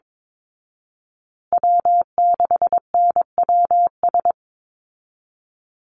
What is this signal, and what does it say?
Morse code, heard as a keyed tone, about 22 wpm, 700 Hz. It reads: W6NWS
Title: Call Signs Repeated 3x - 22wpm